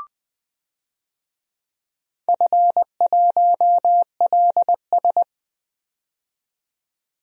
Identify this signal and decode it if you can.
Morse code, heard as a keyed tone, about 20 wpm, 700 Hz. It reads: F1LS
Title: DXCC-100 Call Signs - 4 Characters - 20wpm